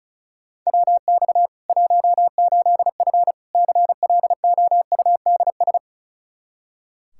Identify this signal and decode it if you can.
Morse code, heard as a keyed tone, about 34 wpm, 700 Hz. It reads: WX 18F CLOUDS
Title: Practice QSO - Exchange 2 - Info - 35wpm